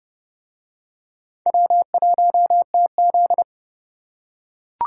Morse code, heard as a keyed tone, about 30 wpm, 700 Hz. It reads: W1TZ